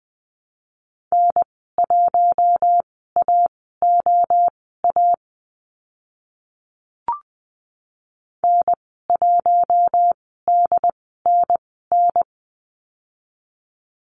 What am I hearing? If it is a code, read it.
Morse code, heard as a keyed tone, about 20 wpm, spaced as for 14 wpm, 700 Hz. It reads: N1AOA N1DNN
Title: 076 Lesson 13g - Letter L - Call Signs 20wpm